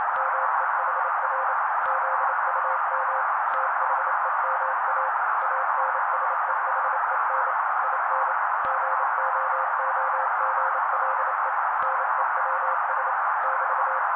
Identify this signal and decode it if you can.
Morse code, heard as a keyed tone, about 28 wpm, 530 Hz. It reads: MEHR ZUM THEMA ANSEHEN IN GOOGLE NEWS <BT>
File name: daily / 27wpm noise